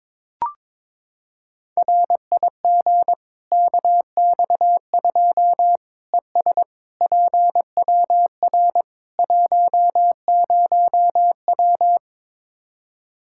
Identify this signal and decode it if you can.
Morse code, heard as a keyed tone, about 22 wpm, 700 Hz. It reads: RIG KX2 ES PWR 10W
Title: Practice QSO - Exchange 2 - Info - 22wpm